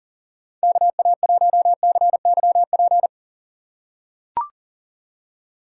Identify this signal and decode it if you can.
Morse code, heard as a keyed tone, about 40 wpm, 700 Hz. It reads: KA1CYP
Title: USA Call Signs - 6 Characters - 2x Character Spacing - 40wpm